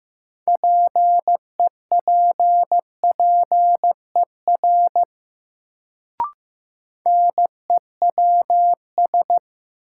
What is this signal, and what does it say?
Morse code, heard as a keyed tone, about 15 wpm, 700 Hz. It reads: PEPPER NEWS
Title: Sets of 5 Words - New Word Drill - 15wpm